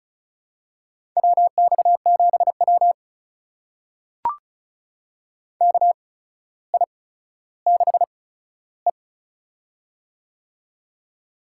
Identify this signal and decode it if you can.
Morse code, heard as a keyed tone, about 36 wpm, spaced as for 15 wpm, 700 Hz. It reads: WX7W KI6E
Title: USA Call Signs - 4 Characters - 8x Character Spacing - 35wpm